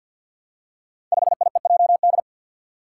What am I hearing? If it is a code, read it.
Morse code, heard as a keyed tone, about 47 wpm, 700 Hz. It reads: 5IEJD
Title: Five Challenging Letter-Number Combinations 50wpm